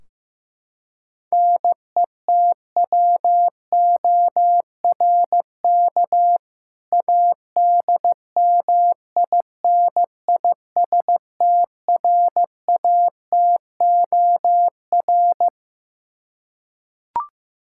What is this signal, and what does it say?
Morse code, heard as a keyed tone, about 15 wpm, 700 Hz. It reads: NETWORK ADMINISTRATOR
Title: K1USN SST QSO Practice 15wpm